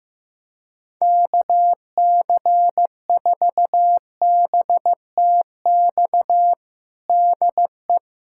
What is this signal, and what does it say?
Morse code, heard as a keyed tone, about 15 wpm, 700 Hz. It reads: KC4BTX DE
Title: USA Callsign DE Callsign - 6 Characters - 15wpm